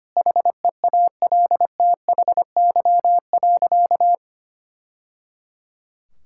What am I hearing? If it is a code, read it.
Morse code, heard as a keyed tone, about 25 wpm, 700 Hz. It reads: HEALTHY.
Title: Sentences from Top 1000 Words - Review Part 5 - 25wpm